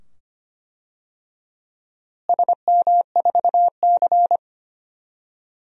Morse code, heard as a keyed tone, about 25 wpm, 700 Hz. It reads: SM4C
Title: DXCC-100 Call Signs - 4 Characters - Repeated 2x - 25wpm